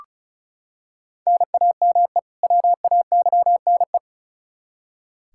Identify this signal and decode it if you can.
Morse code, heard as a keyed tone, about 36 wpm, 700 Hz. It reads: NAME WAYNE
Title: Practice QSO - Exchange 1 - Beginning - 35wpm